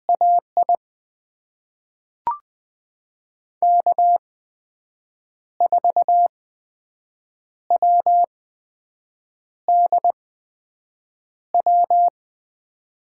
Morse code, heard as a keyed tone, about 20 wpm, spaced as for 6 wpm, 700 Hz. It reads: AI K4WDW
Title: USA Call Signs - 5 Characters - 8x Character Spacing - 20wpm